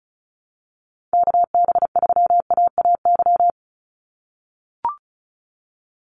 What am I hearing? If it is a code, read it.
Morse code, heard as a keyed tone, about 36 wpm, 700 Hz. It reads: KB3AAY
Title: Call Signs - 35wpm